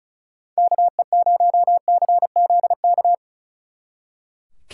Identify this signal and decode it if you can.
Morse code, heard as a keyed tone, about 33 wpm, 700 Hz. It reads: KE0CZK
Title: Speed Racing - USA 2x3 Call Signs 35wpm